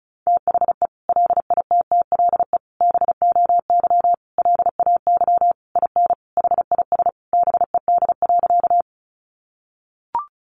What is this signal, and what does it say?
Morse code, heard as a keyed tone, about 34 wpm, 700 Hz. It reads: THE LITTLE BOY LAY IN HIS BED.
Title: Sentences from Top 500 Words 35wpm